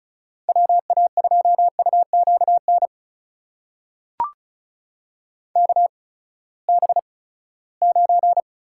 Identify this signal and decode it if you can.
Morse code, heard as a keyed tone, about 36 wpm, spaced as for 20 wpm, 700 Hz. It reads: WA2UQN KB9
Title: USA Call Signs - 6 Characters - 8x Character Spacing - 35wpm